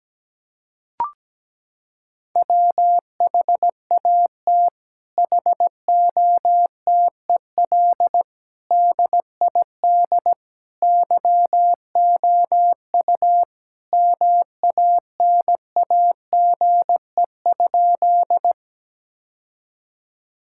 Morse code, heard as a keyed tone, about 17 wpm, 700 Hz. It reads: WHAT HOTEL DID YOU MANAGE?
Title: Sentences from Top 1600 Words 17wpm